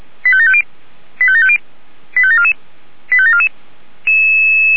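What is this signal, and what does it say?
Five-tone selective call (ZVEI), heard as quick runs of five tones, tones of about 70 ms. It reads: 85750 85750 85744 85744
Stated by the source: ZVEI five-tone selective call, address 85754, received on an SDR.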